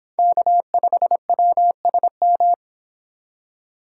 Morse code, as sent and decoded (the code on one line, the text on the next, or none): -.- ..... .-- ... --
K5WSM